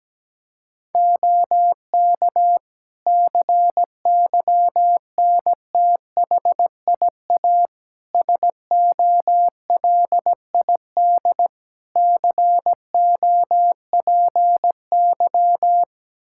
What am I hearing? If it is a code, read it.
Morse code, heard as a keyed tone, about 17 wpm, 700 Hz. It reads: OK CYNTHIA SOLID COPY
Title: K1USN SST QSO Practice 17wpm